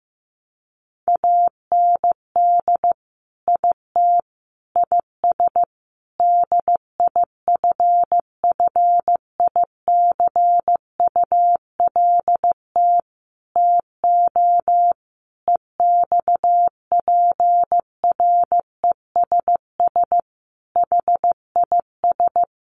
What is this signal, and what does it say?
Morse code, heard as a keyed tone, about 15 wpm, 700 Hz. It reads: AND IT IS DIFFICULT TO EXPRESS HIS
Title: Sentences from Top 1100 Words 15wpm